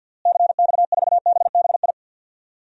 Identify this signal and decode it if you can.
Morse code, heard as a keyed tone, about 53 wpm, 700 Hz. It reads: CX4BBI
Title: DXCC-100 Call Signs - 6 Characters - Repeated 2x - 50wpm